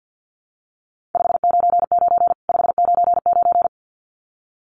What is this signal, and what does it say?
Morse code, heard as a keyed tone, about 50 wpm, 700 Hz. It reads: 599 599
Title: Common QSO Phrases 50wpm